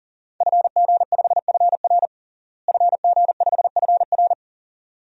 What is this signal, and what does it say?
Morse code, heard as a keyed tone, about 40 wpm, 700 Hz. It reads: FG5FR FG5FR
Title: DXCC-100 Call Signs - 5 Characters - Repeated 2x - 40wpm